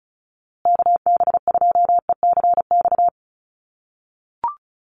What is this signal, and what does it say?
Morse code, heard as a keyed tone, about 34 wpm, 700 Hz. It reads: KB2ECX
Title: USA Call Signs Repeated 3x 35wpm